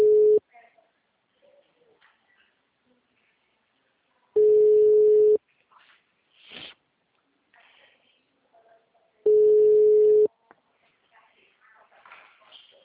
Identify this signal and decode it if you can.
Ringback tone (Europe, 425 Hz), with speech.